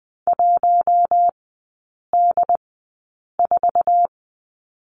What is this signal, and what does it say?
Morse code, heard as a keyed tone, about 20 wpm, spaced as for 8 wpm, 700 Hz. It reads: 1D4
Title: 184 Lesson 30e - Letter b - Three Characters 20wpm